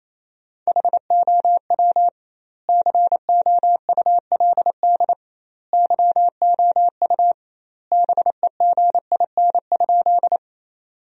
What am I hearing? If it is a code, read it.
Morse code, heard as a keyed tone, about 28 wpm, 700 Hz. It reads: HOW COULD YOU BEGIN?
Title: 2x Word-Spacing - Sentences from Top 300 Words - 28wpm